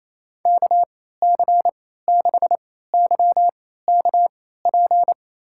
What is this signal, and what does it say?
Morse code, heard as a keyed tone, about 28 wpm, spaced as for 16 wpm, 700 Hz. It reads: KC6YKP
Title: USA Call Signs - 3x Character Spacing - 28wpm